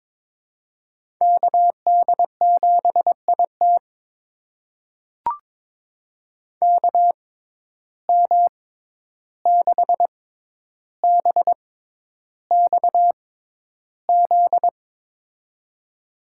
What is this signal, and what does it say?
Morse code, heard as a keyed tone, about 22 wpm, spaced as for 11 wpm, 700 Hz. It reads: KD7IT KM6BXZ
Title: USA Call Signs - 6x Character Spacing - 22wpm